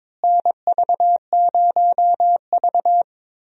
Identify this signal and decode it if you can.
Morse code, heard as a keyed tone, about 22 wpm, 700 Hz. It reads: NV0V
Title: USA Call Signs - 4 Characters - Repeated 2x - 22wpm